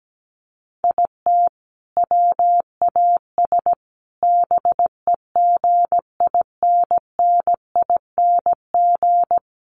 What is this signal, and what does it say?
Morse code, heard as a keyed tone, about 17 wpm, 700 Hz. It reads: IT WAS BEGINNING
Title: Sentences from Top 800 Words 17wpm